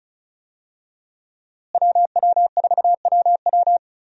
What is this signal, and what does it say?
Morse code, heard as a keyed tone, about 34 wpm, 700 Hz. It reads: WW4WW
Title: USA Call Signs - 5 Characters - Repeated 2x - 35wpm